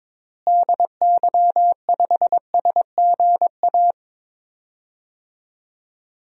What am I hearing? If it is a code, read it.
Morse code, heard as a keyed tone, about 22 wpm, 700 Hz. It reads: DY5SGA
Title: DXCC-100 Call Signs - 6 Characters - Repeated 2x - 22wpm